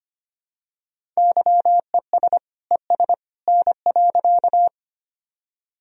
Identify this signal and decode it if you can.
Morse code, heard as a keyed tone, about 25 wpm, 700 Hz. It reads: YES ES N.
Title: Sentences from Top 500 Words and Abbreviations 25wpm